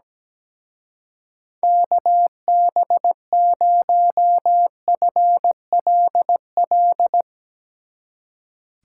Morse code, heard as a keyed tone, about 17 wpm, 700 Hz. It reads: KB0FLL
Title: Speed Racing - USA 2x3 Call Signs 17wpm